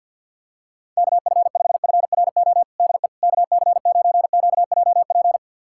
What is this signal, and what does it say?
Morse code, heard as a keyed tone, about 47 wpm, 700 Hz. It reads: KF5FRO DE KC9QJP
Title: USA Callsign DE Callsign - 6 Characters - 50wpm